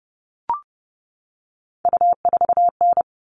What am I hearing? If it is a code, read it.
Morse code, heard as a keyed tone, about 30 wpm, 700 Hz. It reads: U4N
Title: All Cut Numbers - 3-Digit 30wpm